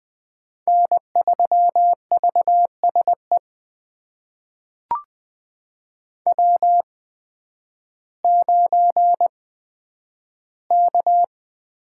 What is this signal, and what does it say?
Morse code, heard as a keyed tone, about 20 wpm, spaced as for 11 wpm, 700 Hz. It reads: N3VSE W9K
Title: USA Call Signs - 8x Character Spacing - 20wpm